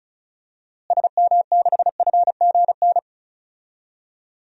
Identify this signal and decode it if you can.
Morse code, heard as a keyed tone, about 36 wpm, 700 Hz. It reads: SM6FGN
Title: DXCC-100 Call Signs - 6 Characters - Repeated 3x - 35wpm